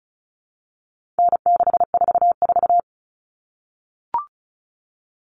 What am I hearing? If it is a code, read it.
Morse code, heard as a keyed tone, about 34 wpm, 700 Hz. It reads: N644
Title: Common Cut Numbers - 4-Digit 35wpm